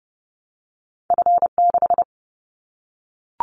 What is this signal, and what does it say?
Morse code, heard as a keyed tone, about 30 wpm, 700 Hz. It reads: F6
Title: Callsign Prefixes for Top 25 DXCC - 30wpm